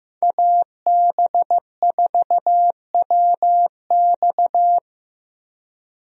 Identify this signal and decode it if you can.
Morse code, heard as a keyed tone, about 15 wpm, 700 Hz. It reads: AB4WX
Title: USA Call Signs - 5 Characters - Repeated 3x - 15wpm